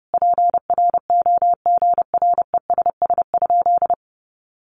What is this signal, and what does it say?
Morse code, heard as a keyed tone, about 30 wpm, 700 Hz. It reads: PROGRESS?
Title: Sentences from Top 1400 Words 30wpm